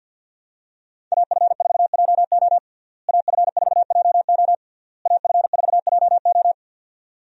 Morse code, heard as a keyed tone, about 48 wpm, 700 Hz. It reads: AF4JO AF4JO AF4JO
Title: USA Call Signs - 5 Characters - Repeated 3x - 50wpm